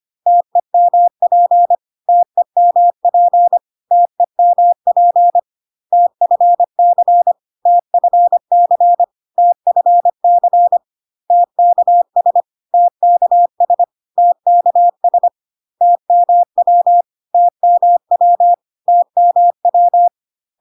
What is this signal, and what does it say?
Morse code, heard as a keyed tone, about 25 wpm, 700 Hz. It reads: TEMP TEMP TEMP TFC TFC TFC TKS TKS TKS TMW TMW TMW
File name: CW abbreviations alphabetical repeated 3 times 25 25wpm